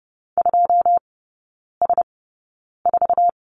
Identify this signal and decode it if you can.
Morse code, heard as a keyed tone, about 30 wpm, spaced as for 9 wpm, 700 Hz. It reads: 2S4
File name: Lesson 17e - Letter u - Three Characters 30wpm